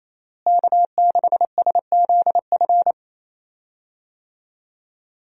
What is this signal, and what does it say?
Morse code, heard as a keyed tone, about 28 wpm, 700 Hz. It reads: K6SZF
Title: USA Call Signs - 5 Characters - Repeated 3x - 28wpm